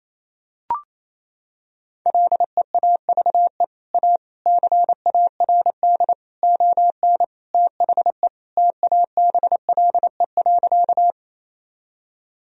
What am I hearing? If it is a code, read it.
Morse code, heard as a keyed tone, about 28 wpm, 700 Hz. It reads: LEAVE A CARD ON THE TABLE.
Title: Sentences from Top 1000 Words - Review Part 4 - 28wpm